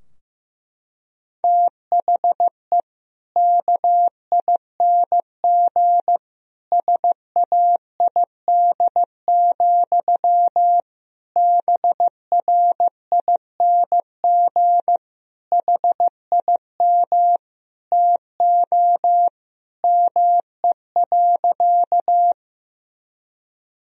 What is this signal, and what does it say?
Morse code, heard as a keyed tone, about 16 wpm, 700 Hz. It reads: THE KING SAID, BRING HIM TO ME.
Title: Sentences from Top 1000 Words - Review Part 3 - 15wpm